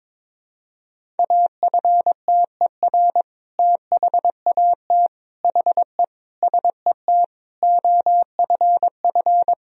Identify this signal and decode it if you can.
Morse code, heard as a keyed tone, about 22 wpm, 700 Hz. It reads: AFTER THAT HE SET OFF
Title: Sentences from Top 1000 Words - Review Part 1 - 22wpm